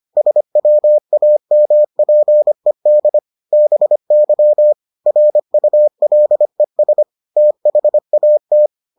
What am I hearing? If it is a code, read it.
Morse code, heard as a keyed tone, about 25 wpm, 600 Hz. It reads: SWAMPED BY RULES THAT